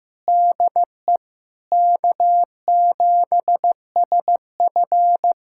Morse code, heard as a keyed tone, about 15 wpm, 700 Hz. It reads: DE K7SF
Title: Practice QSO - Exchange 4 - Ending - 15wpm